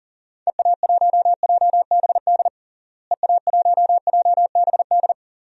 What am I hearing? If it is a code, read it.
Morse code, heard as a keyed tone, about 40 wpm, 700 Hz. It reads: EA1JBD EA1JBD
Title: DXCC-100 Call Signs - 6 Characters - Repeated 2x - 40wpm